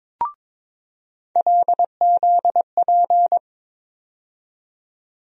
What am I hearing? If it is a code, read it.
Morse code, heard as a keyed tone, about 22 wpm, 700 Hz. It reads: LZP
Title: USA Call Sign Postfixes - 3 Letters 22wpm